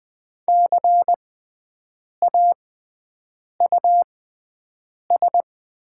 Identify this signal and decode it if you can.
Morse code, heard as a keyed tone, about 20 wpm, spaced as for 7 wpm, 700 Hz. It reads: CAUS
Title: 6x Character Spacing - Top 200 Words - 20wpm